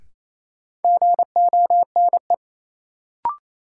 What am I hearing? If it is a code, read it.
Morse code, heard as a keyed tone, about 28 wpm, 700 Hz. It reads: GONE
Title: Sentences from Top 1200 Words - New Word Drill Alt - 28wpm